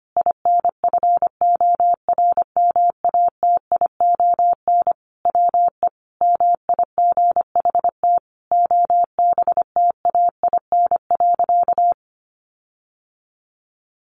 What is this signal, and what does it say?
Morse code, heard as a keyed tone, about 25 wpm, 700 Hz. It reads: INFORMATION WE MIGHT OBTAIN.
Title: Sentences from Top 1300 Words 25wpm